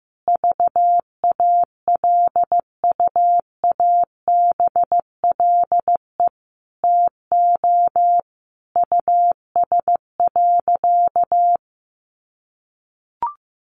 Morse code, heard as a keyed tone, about 15 wpm, 700 Hz. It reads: VALUABLE TO US.
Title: Sentences from Top 1100 Words 15wpm